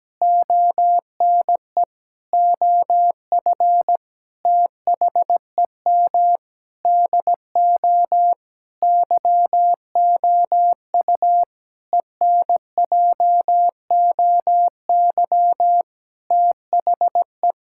Morse code, heard as a keyed tone, about 17 wpm, 700 Hz. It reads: ONE OF THEM DO YOU ENJOY THE
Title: Sentences from Top 1000 Words - Review Part 5 - 17wpm